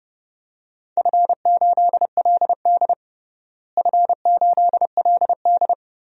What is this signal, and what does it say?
Morse code, heard as a keyed tone, about 30 wpm, 700 Hz. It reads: F8LD F8LD
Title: DXCC-100 Call Signs - 4 Characters - Repeated 2x - 30wpm